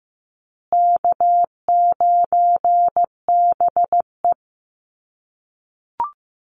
Morse code, heard as a keyed tone, about 15 wpm, 700 Hz. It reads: K9BE
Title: USA Call Signs Repeated 2x 15wpm